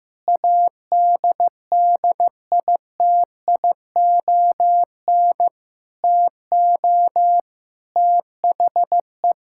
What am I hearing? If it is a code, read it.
Morse code, heard as a keyed tone, about 14 wpm, 700 Hz. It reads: ADDITION TO THE
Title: Sentences from Top 1000 Words - Review Part 6 - 15wpm